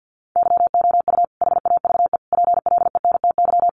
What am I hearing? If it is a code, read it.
Morse code, heard as a keyed tone, about 50 wpm, 700 Hz. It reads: YOU HAVE PLENTY
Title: Sentences from Top 1300 Words 50wpm